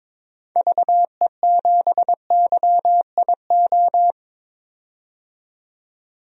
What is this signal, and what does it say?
Morse code, heard as a keyed tone, about 22 wpm, 700 Hz. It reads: VE7YIO
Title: DXCC-100 Call Signs - 6 Characters - 22wpm